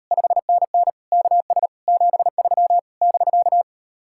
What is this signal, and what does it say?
Morse code, heard as a keyed tone, about 37 wpm, 700 Hz. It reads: 5NN KS 73 <BK>